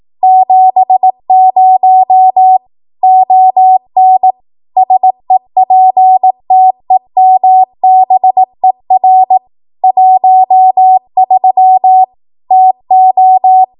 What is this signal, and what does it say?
Morse code, heard as a keyed tone, about 18 wpm, 750 Hz. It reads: 70 ON SEPTEMBER 13 TO
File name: Archive / 18 WPM / 18WPM